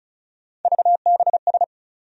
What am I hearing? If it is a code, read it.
Morse code, heard as a keyed tone, about 36 wpm, 700 Hz. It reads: VBS
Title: USA Call Sign Postfixes - 3 Letters 35wpm